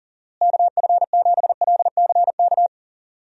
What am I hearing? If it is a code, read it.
Morse code, heard as a keyed tone, about 40 wpm, 700 Hz. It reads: KF7LCK
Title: USA Call Signs - 6 Characters - Repeated 3x - 40wpm